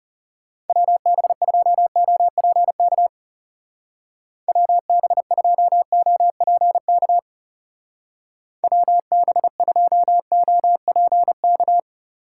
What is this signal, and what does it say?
Morse code, heard as a keyed tone, about 34 wpm, 700 Hz. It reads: WB2OPK WB2OPK WB2OPK
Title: Speed Racing - USA Call Signs 30wpm